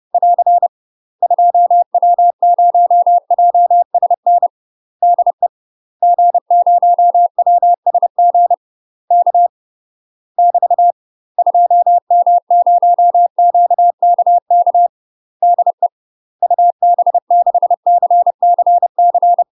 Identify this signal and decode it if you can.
Morse code, heard as a keyed tone, about 30 wpm, 700 Hz. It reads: <AR> 2W0JSN DE G0WSG K <BT> 2M0QKK DE UB6CCC
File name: UK MASTER QSOs TRAINING / 30WPM / UK MASTER QSOs 0038